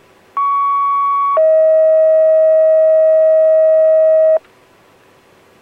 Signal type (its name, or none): two-tone page